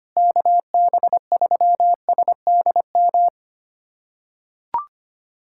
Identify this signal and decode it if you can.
Morse code, heard as a keyed tone, about 25 wpm, 700 Hz. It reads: KB3SDM
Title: USA Callsign DE Callsign - 6 Characters - 25wpm